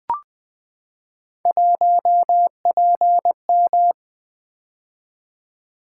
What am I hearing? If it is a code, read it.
Morse code, heard as a keyed tone, about 20 wpm, 700 Hz. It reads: 1PM
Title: Three Letter-Number Combinations - 20wpm